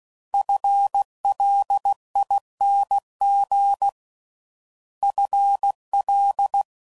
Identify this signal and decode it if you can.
Morse code, heard as a keyed tone, about 16 wpm, 800 Hz. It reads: FLING FL
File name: Prefixes / FLs & THs / FL Prefixes 1